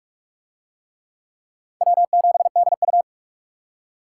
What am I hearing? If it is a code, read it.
Morse code, heard as a keyed tone, about 44 wpm, 700 Hz. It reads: W7DU